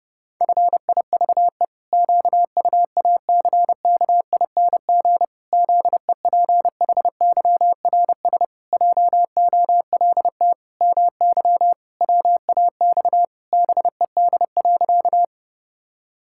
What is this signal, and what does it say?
Morse code, heard as a keyed tone, about 32 wpm, 700 Hz. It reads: FIVE QUACKING ZEPHYRS JOLT MY WAX BED.
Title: Pangram Challenge v2 - 30wpm